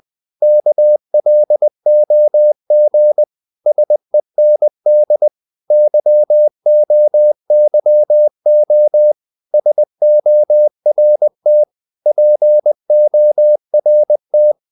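Morse code, heard as a keyed tone, about 20 wpm, 600 Hz. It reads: KLOG SEND YOYO SORT PORT
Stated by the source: Morse code practice of short Danish words at 20 wpm, from CW med Gnister.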